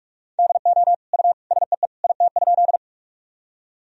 Morse code, heard as a keyed tone, about 45 wpm, 700 Hz. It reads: DO U SEE IT?
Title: Sentences from Top 100 Words and Abbreviations 45wpm